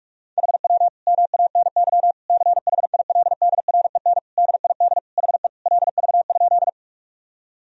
Morse code, heard as a keyed tone, about 46 wpm, 700 Hz. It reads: HW MANY CHILDREN DID HE LV?
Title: Sentences from Top 400 Words and Abbreviations 45wpm